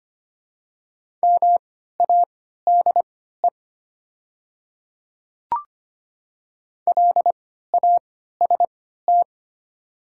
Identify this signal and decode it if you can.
Morse code, heard as a keyed tone, about 25 wpm, spaced as for 14 wpm, 700 Hz. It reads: MADE LAST
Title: 3x Character Spacing - Top 300 Words - 25wpm